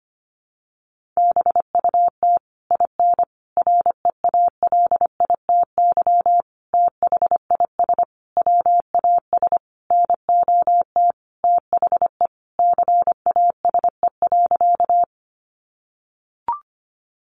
Morse code, heard as a keyed tone, about 25 wpm, 700 Hz. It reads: BUT IN REALITY THIS WAS NOT THE CASE.
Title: Sentences from Top 1200 Words 25wpm